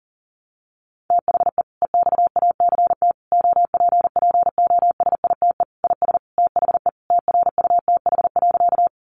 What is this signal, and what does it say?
Morse code, heard as a keyed tone, about 40 wpm, 700 Hz. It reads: THE EXACT OPPOSITE IS THE TRUTH.
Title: Sentences from Top 1300 Words 40wpm